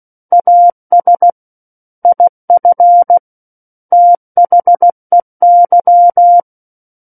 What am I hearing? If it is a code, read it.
Morse code, heard as a keyed tone, about 16 wpm, 700 Hz. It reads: AS IF THEY
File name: Book One / 07-Chapter 7 - 16 WPM